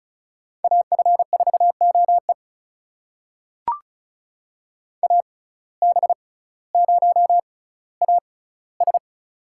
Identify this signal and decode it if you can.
Morse code, heard as a keyed tone, about 34 wpm, spaced as for 18 wpm, 700 Hz. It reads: AF4OE AB0AS
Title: USA Call Signs - 5 Characters - 6x Character Spacing - 35wpm